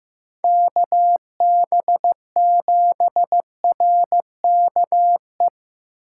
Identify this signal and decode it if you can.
Morse code, heard as a keyed tone, about 15 wpm, 700 Hz. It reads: KB7RKE
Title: USA Call Signs - 8x Character Spacing - 15wpm